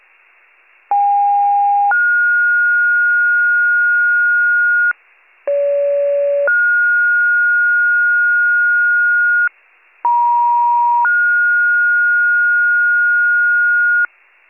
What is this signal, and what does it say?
Two-tone page, heard as a one-second tone, then a three-second tone, about 810 then 1430 Hz, about 570 then 1430 Hz, about 950 then 1430 Hz. It is heard from 0.9 to 14.1 s.